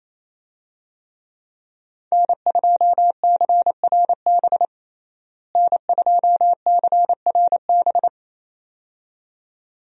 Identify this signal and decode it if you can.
Morse code, heard as a keyed tone, about 27 wpm, 700 Hz. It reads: N2CRB N2CRB
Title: USA Call Signs - 5 Characters - Repeated 2x - 28wpm